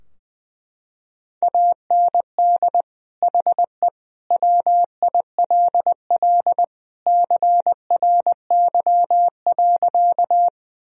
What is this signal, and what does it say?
Morse code, heard as a keyed tone, about 21 wpm, 700 Hz. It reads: AND HE WILL CRY.
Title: Sentences from Top 1000 Words - Review Part 2 - 20wpm